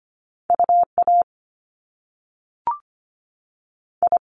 Morse code, heard as a keyed tone, about 25 wpm, spaced as for 3 wpm, 700 Hz. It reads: U AI
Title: Top Two Letter Combinations in Words 25wpm